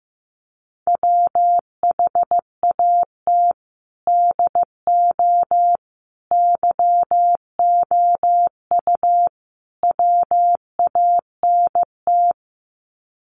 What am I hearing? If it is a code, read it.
Morse code, heard as a keyed tone, about 15 wpm, 700 Hz. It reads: WHAT DO YOU WANT
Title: Sets of 4 Words 15wpm